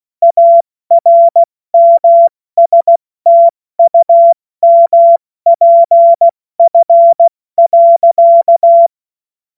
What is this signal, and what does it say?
Morse code, heard as a keyed tone, about 16 wpm, 680 Hz. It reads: ARMSTUMPF.
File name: Morsekurs / Witze / Witz 02. August 2022 70 F80